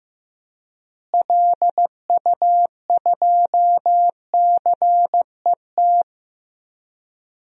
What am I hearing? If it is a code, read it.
Morse code, heard as a keyed tone, about 15 wpm, 700 Hz. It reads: LU2CET
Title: DXCC-100 Call Signs - 6 Characters - Repeated 2x - 15wpm